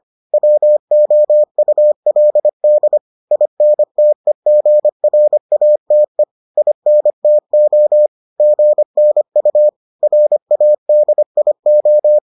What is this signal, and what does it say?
Morse code, heard as a keyed tone, about 25 wpm, 600 Hz. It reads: WOULD INTEGRATE INTO GNU RADIO